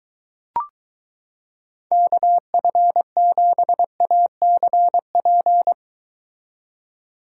Morse code, heard as a keyed tone, about 23 wpm, 700 Hz. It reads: KF7ACP